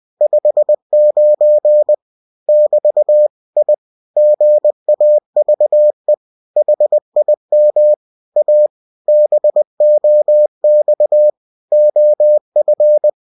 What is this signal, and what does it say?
Morse code, heard as a keyed tone, about 20 wpm, 600 Hz. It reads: 59 <BT> I GAVE HIM A BOX OF